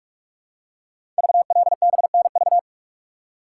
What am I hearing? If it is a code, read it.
Morse code, heard as a keyed tone, about 44 wpm, 690 Hz. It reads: VLBNV
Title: Code Groups - 5 Letter:Numbers - 45wpm